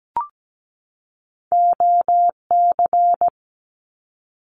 Morse code, heard as a keyed tone, about 17 wpm, 700 Hz. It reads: OC
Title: Abbreviations 17wpm